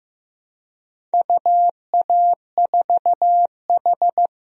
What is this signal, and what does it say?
Morse code, heard as a keyed tone, about 15 wpm, 700 Hz. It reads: UA4H
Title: DXCC-100 Call Signs - 4 Characters - 15wpm